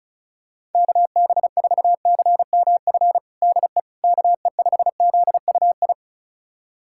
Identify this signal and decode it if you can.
Morse code, heard as a keyed tone, about 36 wpm, 700 Hz. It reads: KB4CMF DE KE5ZUI
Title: USA Callsign DE Callsign 35wpm